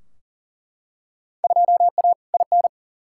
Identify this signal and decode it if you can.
Morse code, heard as a keyed tone, about 40 wpm, 700 Hz. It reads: 2A IN
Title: Field Day Exchanges - 40wpm